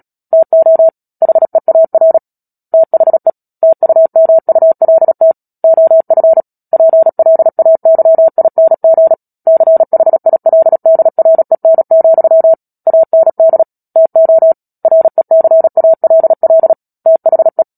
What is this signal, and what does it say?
Morse code, heard as a keyed tone, about 37 wpm, 650 Hz. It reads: TO HEAR THE TUMULT OF PLAYING CHILDREN, AND TO RECALL THE
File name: Book Two / 10-Chapter 10 - Epilogue - 36 WPM